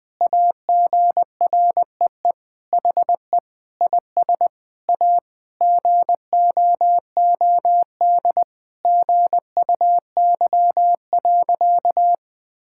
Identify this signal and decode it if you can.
Morse code, heard as a keyed tone, about 21 wpm, 700 Hz. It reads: AGREE HE IS A GOOD GUY.
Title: Sentences with 8-Word Phrases - 20wpm